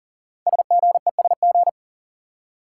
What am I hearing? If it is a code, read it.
Morse code, heard as a keyed tone, about 42 wpm, 690 Hz. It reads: SGESG
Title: Code Groups - 5 Letters - 40wpm